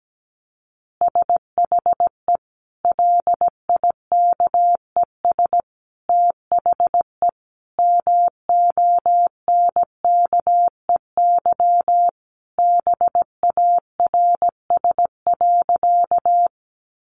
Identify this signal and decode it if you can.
Morse code, heard as a keyed tone, about 17 wpm, 700 Hz. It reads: SHE LIKES THE MONKEY BARS.
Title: Easy Phrases 17wpm